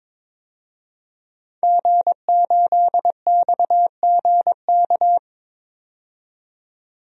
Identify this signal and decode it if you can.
Morse code, heard as a keyed tone, about 22 wpm, 700 Hz. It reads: G8XGK
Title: DXCC-100 Call Signs - 5 Characters - 22wpm